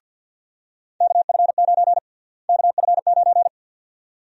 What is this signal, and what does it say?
Morse code, heard as a keyed tone, about 48 wpm, 700 Hz. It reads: KF9 KF9
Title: USA Call Sign Prefixes - Repeated 2x - 50wpm